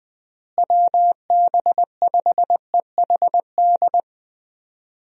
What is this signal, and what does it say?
Morse code, heard as a keyed tone, about 20 wpm, 700 Hz. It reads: WB5EHD
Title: USA Call Signs - 6 Characters - Repeated 2x - 20wpm